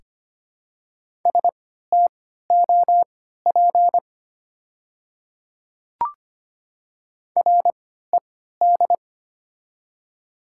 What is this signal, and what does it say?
Morse code, heard as a keyed tone, about 25 wpm, spaced as for 14 wpm, 700 Hz. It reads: STOP RED
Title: 3x Character Spacing - Top 500 Words - 25wpm